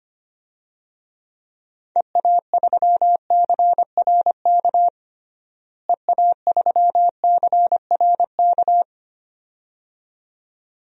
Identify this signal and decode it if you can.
Morse code, heard as a keyed tone, about 25 wpm, 700 Hz. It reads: EA3CRK EA3CRK
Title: DXCC-100 Call Signs - 6 Characters - Repeated 2x - 25wpm